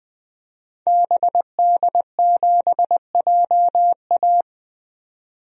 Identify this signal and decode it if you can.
Morse code, heard as a keyed tone, about 20 wpm, 700 Hz. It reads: BD7JA
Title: DXCC-100 Call Signs - 5 Characters - 20wpm